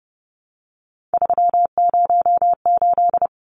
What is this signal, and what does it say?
Morse code, heard as a keyed tone, about 30 wpm, 700 Hz. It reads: 308
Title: Numbers - 3-Digit 30wpm